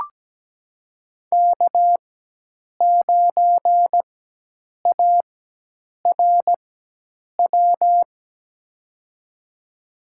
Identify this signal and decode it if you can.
Morse code, heard as a keyed tone, about 17 wpm, spaced as for 8 wpm, 700 Hz. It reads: K9ARW
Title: USA Call Signs - 4x Character Spacing - 17wpm